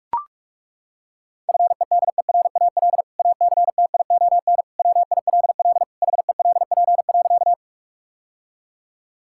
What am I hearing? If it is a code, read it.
Morse code, heard as a keyed tone, about 42 wpm, 700 Hz. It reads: FEDERAL ACTION WILL HELP.
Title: Sentences from Top 1000 Words - Review Part 5 - 45wpm